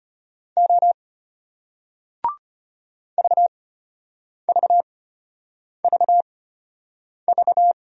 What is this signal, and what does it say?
Morse code, heard as a keyed tone, about 34 wpm, spaced as for 8 wpm, 700 Hz. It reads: O VVVV